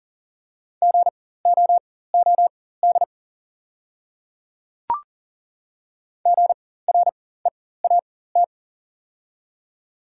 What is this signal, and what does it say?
Morse code, heard as a keyed tone, about 40 wpm, spaced as for 19 wpm, 690 Hz. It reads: GOOD GREAT
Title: 4x Character Spacing - Top 300 Words - 40wpm